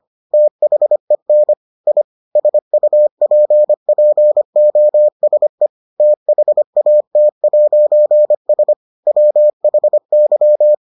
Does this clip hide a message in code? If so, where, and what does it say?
Yes: Morse code from 0.3 to 10.7 s, heard as a keyed tone, about 25 wpm, 600 Hz. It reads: THEN I SUPPOSE THAT'S WHY